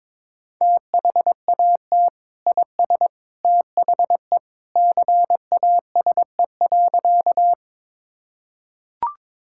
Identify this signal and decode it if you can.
Morse code, heard as a keyed tone, about 22 wpm, 700 Hz. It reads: THAT IS THE CASE.
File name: practice / 22wpm / Sentences with 2-Word Phrases - 22wpm